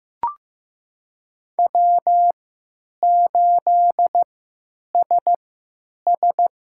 Morse code, heard as a keyed tone, about 15 wpm, spaced as for 9 wpm, 700 Hz. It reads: W8SS
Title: USA Call Signs - 4 Characters - 3x Character Spacing - 15wpm